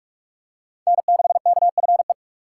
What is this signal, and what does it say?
Morse code, heard as a keyed tone, about 46 wpm, 700 Hz. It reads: N6KFE
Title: USA Call Signs - 5 Characters - 3x Character Spacing - 45wpm